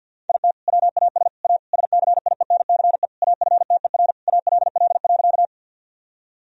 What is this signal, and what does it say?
Morse code, heard as a keyed tone, about 52 wpm, 700 Hz. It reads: IT WAS A SCIENCE AFTER ALL.
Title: Sentences from Top 1000 Words - Review Part 2 - 50wpm